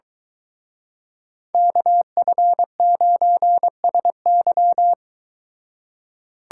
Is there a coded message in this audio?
Yes: Morse code from 1.5 to 4.9 s, heard as a keyed tone, about 22 wpm, 700 Hz. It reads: KF9SY